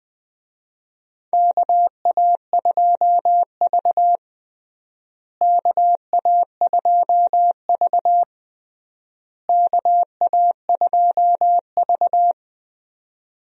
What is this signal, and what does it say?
Morse code, heard as a keyed tone, about 20 wpm, 700 Hz. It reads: KA2V KA2V KA2V
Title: USA Call Signs - 4 Characters - Repeated 3x - 20wpm